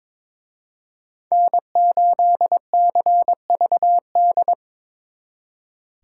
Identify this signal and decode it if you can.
Morse code, heard as a keyed tone, about 26 wpm, 700 Hz. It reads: N8CVD
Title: Speed Racing - USA 1x3 Call Signs 22wpm